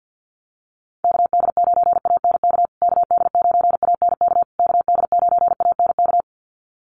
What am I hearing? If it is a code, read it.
Morse code, heard as a keyed tone, about 50 wpm, 700 Hz. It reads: KD9ANK KD9ANK KD9ANK
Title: USA Call Signs Repeated 3x 50wpm